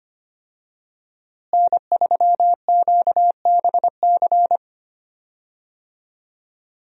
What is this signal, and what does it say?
Morse code, heard as a keyed tone, about 25 wpm, 700 Hz. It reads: N3QBC
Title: USA Call Signs - 5 Characters - Repeated 2x - 25wpm